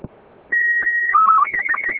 SSTV (slow-scan television), Robot 36.